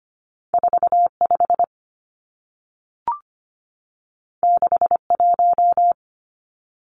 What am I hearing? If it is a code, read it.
Morse code, heard as a keyed tone, about 25 wpm, 700 Hz. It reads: 45 61
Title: Numbers - 2-Digit 25wpm